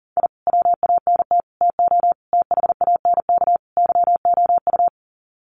Morse code, heard as a keyed tone, about 40 wpm, 700 Hz. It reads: I WANT TO THANK YOU
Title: Sets of 5 Words Encore 40wpm